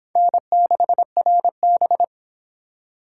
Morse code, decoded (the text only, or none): N6RB